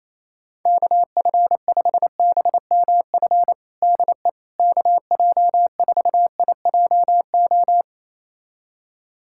Morse code, heard as a keyed tone, about 28 wpm, 700 Hz. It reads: KF5BMF DE KJ4IJO
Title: USA Callsign DE Callsign - 6 Characters - 28wpm